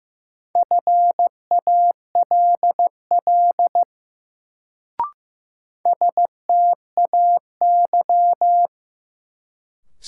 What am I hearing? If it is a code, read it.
Morse code, heard as a keyed tone, about 15 wpm, 700 Hz. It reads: FALL STAY
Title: Sentences from Top 500 Words - New Word Drill - 15wpm